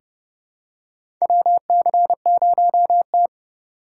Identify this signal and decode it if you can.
Morse code, heard as a keyed tone, about 32 wpm, 700 Hz. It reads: WC0T
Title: USA Call Signs - 3x Character Spacing - 30wpm